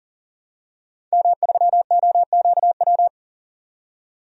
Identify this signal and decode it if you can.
Morse code, heard as a keyed tone, about 40 wpm, 700 Hz. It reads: M3OQW
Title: DXCC-100 Call Signs - 5 Characters - 40wpm